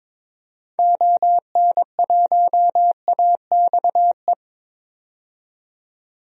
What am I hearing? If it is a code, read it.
Morse code, heard as a keyed tone, about 22 wpm, 700 Hz. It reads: ON1AXE
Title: DXCC-100 Call Signs - 6 Characters - 22wpm